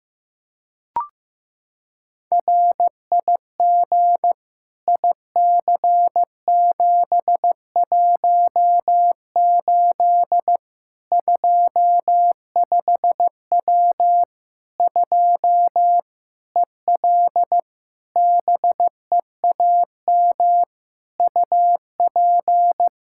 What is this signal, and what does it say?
Morse code, heard as a keyed tone, about 16 wpm, 700 Hz. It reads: RIG IC718 25W 2 EL BEAM UP
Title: Practice QSO - Exchange 2 - Info - 15wpm